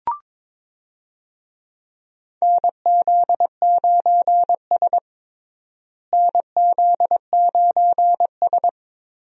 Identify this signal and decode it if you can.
Morse code, heard as a keyed tone, about 22 wpm, 700 Hz. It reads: NZ9S NZ9S
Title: USA Call Signs - 4 Characters - Repeated 3x - 22wpm